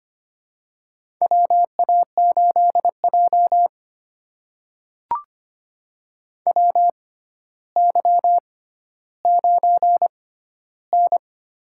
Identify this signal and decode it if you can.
Morse code, heard as a keyed tone, about 25 wpm, spaced as for 13 wpm, 700 Hz. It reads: WA8J WY9N
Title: USA Call Signs - 6x Character Spacing - 25wpm